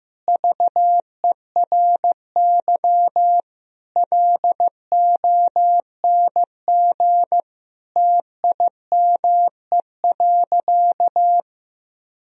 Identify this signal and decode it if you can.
Morse code, heard as a keyed tone, about 15 wpm, 700 Hz. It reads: VERY LONG TIME.
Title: Sentences from Top 1000 Words - Review Part 3 - 15wpm